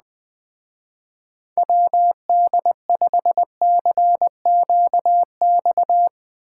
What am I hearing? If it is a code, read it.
Morse code, heard as a keyed tone, about 20 wpm, 700 Hz. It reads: WD5CQX